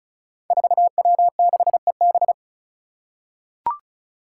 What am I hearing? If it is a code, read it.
Morse code, heard as a keyed tone, about 36 wpm, 700 Hz. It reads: 4W6EB
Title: Five Challenging Letter-Number Combinations 35wpm